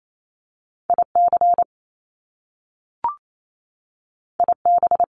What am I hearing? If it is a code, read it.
Morse code, heard as a keyed tone, about 28 wpm, 700 Hz. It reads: IC IB
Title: Top Two Letter Combinations in Words 28wpm